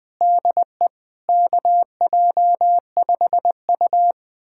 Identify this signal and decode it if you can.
Morse code, heard as a keyed tone, about 20 wpm, 700 Hz. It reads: DE KJ5U
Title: USA Callsign DE Callsign - 4 Characters - 20wpm